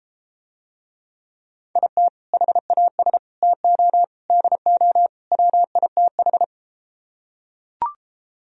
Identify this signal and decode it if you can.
Morse code, heard as a keyed tone, about 34 wpm, 700 Hz. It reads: IT HAS TO DO WITH